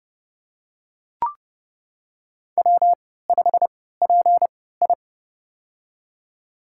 Morse code, heard as a keyed tone, about 30 wpm, spaced as for 17 wpm, 700 Hz. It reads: W5PI
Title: USA Call Signs - 4 Characters - 3x Character Spacing - 30wpm